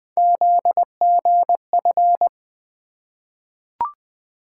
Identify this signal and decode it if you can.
Morse code, heard as a keyed tone, about 20 wpm, 700 Hz. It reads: ZGF
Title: Three Letter-Number Combinations - 20wpm